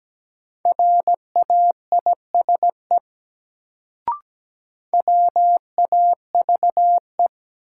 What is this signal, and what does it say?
Morse code, heard as a keyed tone, about 17 wpm, 700 Hz. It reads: RAISE WAVE
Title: Sentences from Top 1400 Words - New Word Drill Alt - 17wpm